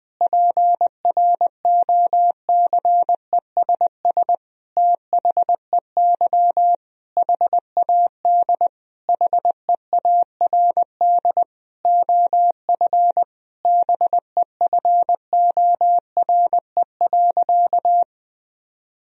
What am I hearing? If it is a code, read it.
Morse code, heard as a keyed tone, about 20 wpm, 700 Hz. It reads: PROCESS THEY HAD HEARD OF BEFORE.
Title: Sentences from Top 1000 Words - Review Part 3 - 20wpm